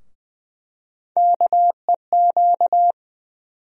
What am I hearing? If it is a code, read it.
Morse code, heard as a keyed tone, about 20 wpm, 700 Hz. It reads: KEQ
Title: USA Call Sign Postfixes - 3 Letters 20wpm